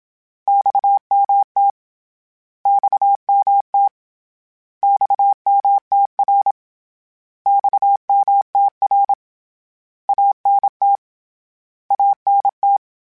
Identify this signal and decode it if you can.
Morse code, heard as a keyed tone, about 27 wpm, 800 Hz. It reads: XMT XMT XMTR XMTR ANT ANT
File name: Abreviations / Abreviations mp3s / Abreviations@25